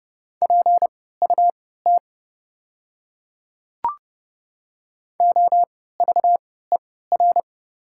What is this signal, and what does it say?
Morse code, heard as a keyed tone, about 30 wpm, spaced as for 17 wpm, 690 Hz. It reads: PUT OVER
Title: 3x Character Spacing - Top 200 Words - 30wpm